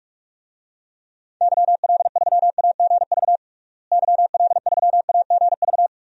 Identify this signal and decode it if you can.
Morse code, heard as a keyed tone, about 46 wpm, 700 Hz. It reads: YL3AGV YL3AGV
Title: DXCC-100 Call Signs - 6 Characters - Repeated 2x - 45wpm